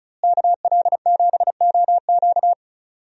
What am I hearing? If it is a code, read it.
Morse code, heard as a keyed tone, about 36 wpm, 700 Hz. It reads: KL7OQ
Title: DXCC-100 Call Signs - 5 Characters - Repeated 3x - 35wpm